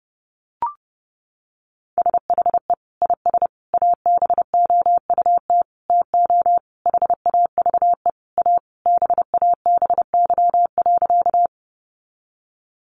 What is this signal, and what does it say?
Morse code, heard as a keyed tone, about 28 wpm, 700 Hz. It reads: SHE IS ABOUT TO HAVE A BABY.
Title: Sentences from Top 1300 Words 30wpm